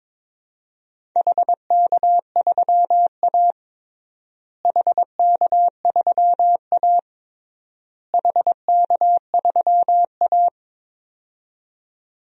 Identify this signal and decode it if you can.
Morse code, heard as a keyed tone, about 22 wpm, 700 Hz. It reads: HK3A HK3A HK3A
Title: DXCC-100 Call Signs - 4 Characters - Repeated 3x - 22wpm